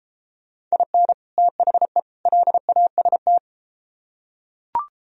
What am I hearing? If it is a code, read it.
Morse code, heard as a keyed tone, about 31 wpm, 700 Hz. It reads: IN THE LAST